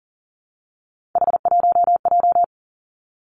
Morse code, heard as a keyed tone, about 40 wpm, 700 Hz. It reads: H1J
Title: Three Challenging Letter-Number Combinations 40wpm